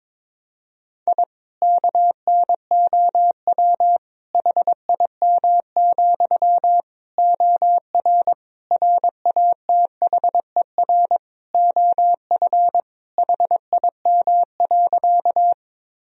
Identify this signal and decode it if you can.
Morse code, heard as a keyed tone, about 21 wpm, 700 Hz. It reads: I KNOW HIM, OR RATHER OF HIM.
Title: Sentences from Top 1000 Words - Review Part 4 - 22wpm